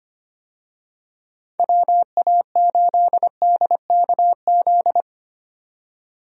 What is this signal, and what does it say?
Morse code, heard as a keyed tone, about 25 wpm, 700 Hz. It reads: WA8DKZ
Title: USA Call Signs - 6 Characters - 25wpm